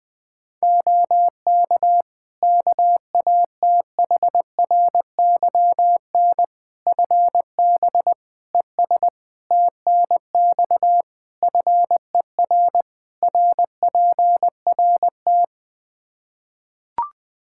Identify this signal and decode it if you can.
Morse code, heard as a keyed tone, about 21 wpm, 700 Hz. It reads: OK KATHRYN FB ES TNX FER RPRT
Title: K1USN SST QSO Practice 20wpm